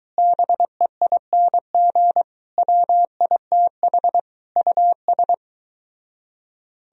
Morse code, heard as a keyed tone, about 23 wpm, 700 Hz. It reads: BEING WITH US